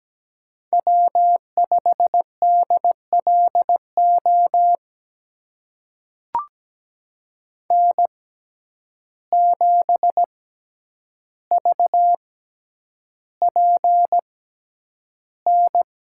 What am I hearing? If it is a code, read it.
Morse code, heard as a keyed tone, about 17 wpm, spaced as for 9 wpm, 700 Hz. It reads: W5DLO N7VPN
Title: USA Call Signs - 5 Characters - 6x Character Spacing - 17wpm